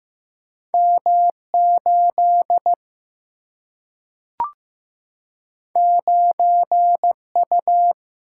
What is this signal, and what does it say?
Morse code, heard as a keyed tone, about 15 wpm, 700 Hz. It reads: M8 9U
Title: Single-Letter Single-Number 15wpm